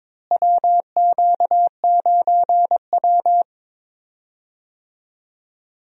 Morse code, heard as a keyed tone, about 22 wpm, 700 Hz. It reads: WQ9W
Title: USA Call Signs - 4 Characters - Repeated 2x - 22wpm